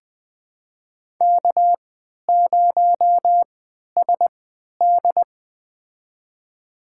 Morse code, heard as a keyed tone, about 21 wpm, spaced as for 12 wpm, 700 Hz. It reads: K0SD
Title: USA Call Signs - 4 Characters - 3x Character Spacing - 20wpm